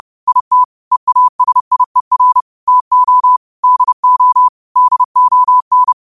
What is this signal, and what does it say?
Morse code, heard as a keyed tone, about 30 wpm, 1000 Hz. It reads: IT EASIER TO DO DON
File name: qotdfiles / 30wpm